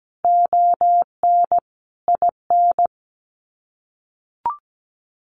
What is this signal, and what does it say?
Morse code, heard as a keyed tone, about 17 wpm, 700 Hz. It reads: ON IN
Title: Sets of 5 Words Encore2 17wpm